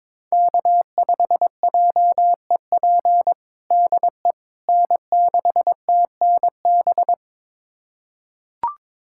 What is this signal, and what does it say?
Morse code, heard as a keyed tone, about 22 wpm, 700 Hz. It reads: K5JEP DE N6TNB
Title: USA Callsign DE Callsign - 5 Characters - 22wpm